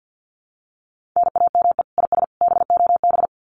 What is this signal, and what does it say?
Morse code, heard as a keyed tone, about 50 wpm, 700 Hz. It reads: NAME IS BOB
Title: Common QSO Phrases 50wpm